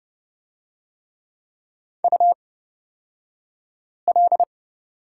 Morse code, heard as a keyed tone, about 30 wpm, spaced as for 5 wpm, 700 Hz. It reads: UL
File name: Lesson 38d - Letter q - Two Characters 30wpm